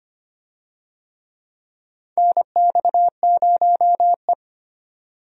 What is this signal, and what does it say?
Morse code, heard as a keyed tone, about 25 wpm, 700 Hz. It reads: NX0E
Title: USA Call Signs - 4 Characters - 25wpm